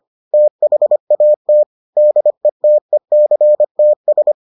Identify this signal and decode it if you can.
Morse code, heard as a keyed tone, about 25 wpm, 600 Hz. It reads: THAT DETECTS